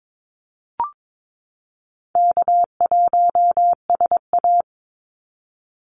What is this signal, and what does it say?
Morse code, heard as a keyed tone, about 22 wpm, 700 Hz. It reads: K1SA
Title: Easy Call Signs Repeated 2x - 22wpm